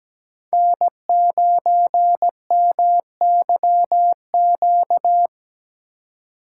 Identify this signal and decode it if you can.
Morse code, heard as a keyed tone, about 17 wpm, 700 Hz. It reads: N9MYQ
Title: USA Call Signs - 5 Characters - 3x Character Spacing - 17wpm